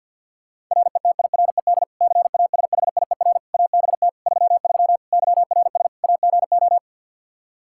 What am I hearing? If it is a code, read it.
Morse code, heard as a keyed tone, about 52 wpm, 700 Hz. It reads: RETIRED CASHIER ABT 33 YRS AGO
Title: Practice QSO - Exchange 3 - More Info - 50wpm